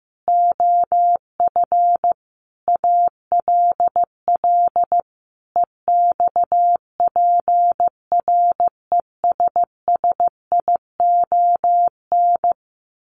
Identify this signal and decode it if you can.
Morse code, heard as a keyed tone, about 15 wpm, 700 Hz. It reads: OF ALL EXPRESSION
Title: A Sound Of Thunder 15wpm